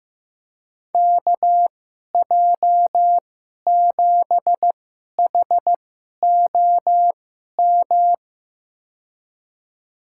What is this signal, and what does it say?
Morse code, heard as a keyed tone, about 15 wpm, spaced as for 11 wpm, 700 Hz. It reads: KJ7HOM
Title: USA Call Signs - 6 Characters - 2x Character Spacing - 15wpm